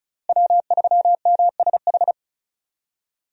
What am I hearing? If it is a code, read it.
Morse code, heard as a keyed tone, about 36 wpm, 700 Hz. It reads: W3MSH
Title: USA Call Signs - 5 Characters - Repeated 3x - 35wpm